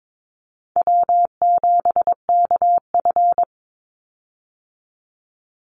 Morse code, heard as a keyed tone, about 22 wpm, 700 Hz. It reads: W7KF
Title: Easy Call Signs - 22wpm